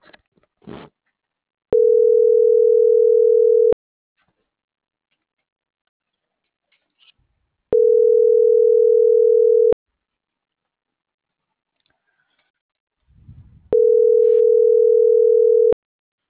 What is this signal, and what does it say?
Ringback tone (North America), 440 + 480 Hz.